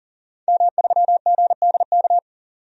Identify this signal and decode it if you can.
Morse code, heard as a keyed tone, about 40 wpm, 700 Hz. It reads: M3GDK
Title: DXCC-100 Call Signs - 5 Characters - 40wpm